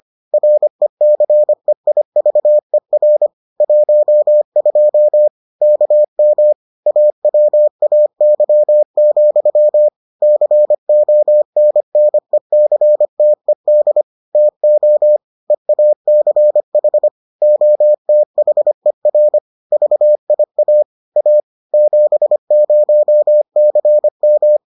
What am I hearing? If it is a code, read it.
Morse code, heard as a keyed tone, about 25 wpm, 600 Hz. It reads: RECEIVER 12 KM AWAY, CONNECTED TO EACH OTHER VIA A 70CM